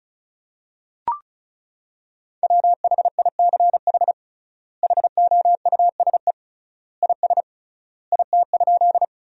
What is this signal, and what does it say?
Morse code, heard as a keyed tone, about 36 wpm, 700 Hz. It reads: WHICH HOUSE IS IT?
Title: 3x Word-Spacing - Sentences from Top 200 Words - 35wpm